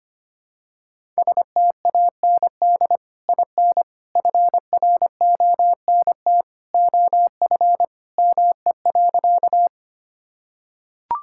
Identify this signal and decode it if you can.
Morse code, heard as a keyed tone, about 25 wpm, 700 Hz. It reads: STAND IN FRONT OF ME.
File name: practice / 25wpm / Sentences with 2-Word Phrases - 25wpm